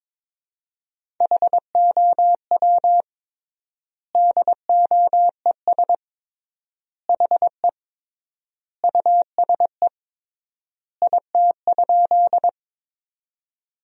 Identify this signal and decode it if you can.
Morse code, heard as a keyed tone, about 22 wpm, 700 Hz. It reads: HOW DOES HE USE IT?
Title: 3x Word-Spacing - Sentences from Top 200 Words - 22wpm